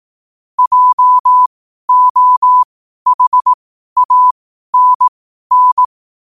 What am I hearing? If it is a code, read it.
Morse code, heard as a keyed tone, about 18 wpm, spaced as for 12 wpm, 1000 Hz. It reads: JOHANN
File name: qotdfiles / 12wpm